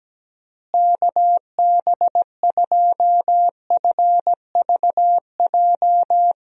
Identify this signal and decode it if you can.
Morse code, heard as a keyed tone, about 17 wpm, 700 Hz. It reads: KB2FVJ
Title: USA Call Signs - 6 Characters - Repeated 2x - 17wpm